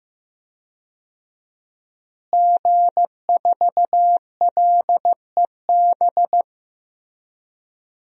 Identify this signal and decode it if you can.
Morse code, heard as a keyed tone, about 15 wpm, 700 Hz. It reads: G4LEB
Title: DXCC-100 Call Signs - 5 Characters - Repeated 3x - 15wpm